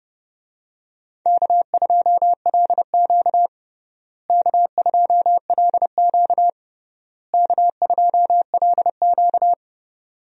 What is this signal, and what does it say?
Morse code, heard as a keyed tone, about 32 wpm, 700 Hz. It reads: K2LQ K2LQ K2LQ
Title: USA Call Signs - 4 Characters - Repeated 3x - 30wpm